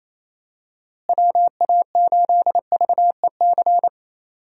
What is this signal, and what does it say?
Morse code, heard as a keyed tone, about 27 wpm, 700 Hz. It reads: WA8VEC
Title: USA Call Signs - 6 Characters - Repeated 2x - 28wpm